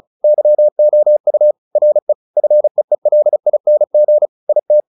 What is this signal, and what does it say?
Morse code, heard as a keyed tone, about 34 wpm, 600 Hz. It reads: YOU RE FEELING IT
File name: the old man and the sea / the old man and the sea 1 35wpm